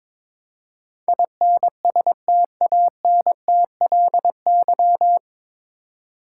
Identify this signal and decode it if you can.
Morse code, heard as a keyed tone, about 22 wpm, 700 Hz. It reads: INSTANTLY
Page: Morse Code Ninja practice 22wpm